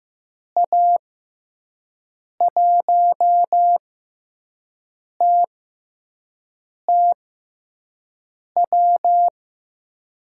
Morse code, heard as a keyed tone, about 15 wpm, spaced as for 5 wpm, 700 Hz. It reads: A1TTW
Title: USA Call Signs - 6 Characters - 6x Character Spacing - 15wpm